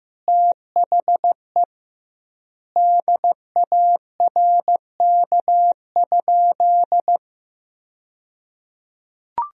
Morse code, heard as a keyed tone, about 16 wpm, 700 Hz. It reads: THE DARK?
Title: 2x Word-Spacing - Sentences from Top 500 Words - 15wpm